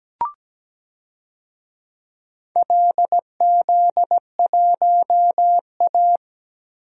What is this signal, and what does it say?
Morse code, heard as a keyed tone, about 17 wpm, 700 Hz. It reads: LZ1A
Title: DXCC-100 Call Signs - 4 Characters - Repeated 3x - 17wpm